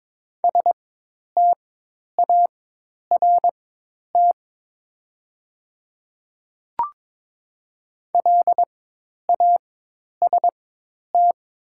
Morse code, heard as a keyed tone, about 22 wpm, spaced as for 10 wpm, 700 Hz. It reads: START LAST
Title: 4x Character Spacing - Top 300 Words - 22wpm